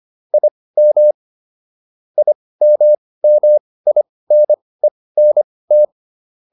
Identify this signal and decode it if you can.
Morse code, heard as a keyed tone, about 25 wpm, spaced as for 18 wpm, 600 Hz. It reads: IM IMMINENT